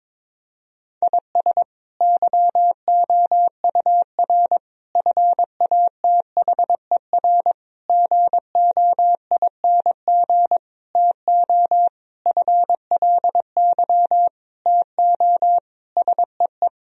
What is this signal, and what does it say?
Morse code, heard as a keyed tone, about 22 wpm, 700 Hz. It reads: IS YOUR FATHER GOING TO FLY TO SEE
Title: Sentences from Top 1000 Words - Review Part 2 - 22wpm